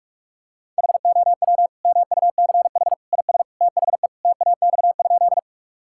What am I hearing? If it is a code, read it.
Morse code, heard as a keyed tone, about 46 wpm, 700 Hz. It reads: HOW MUCH IS THE TAX?
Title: Sentences from Top 1000 Words - Review Part 5 - 45wpm